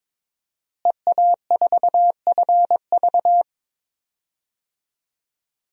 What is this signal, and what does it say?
Morse code, heard as a keyed tone, about 22 wpm, 700 Hz. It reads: EA4FV
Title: DXCC-100 Call Signs - 5 Characters - Repeated 2x - 22wpm